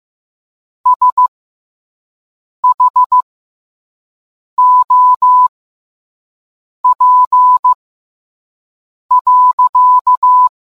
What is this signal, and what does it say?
Morse code, heard as a keyed tone, about 15 wpm, spaced as for 5 wpm, 1000 Hz. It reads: SHOP.